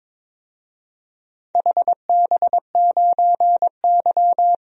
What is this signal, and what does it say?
Morse code, heard as a keyed tone, about 22 wpm, 700 Hz. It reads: HB9Y
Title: DXCC-100 Call Signs - 4 Characters - Repeated 3x - 22wpm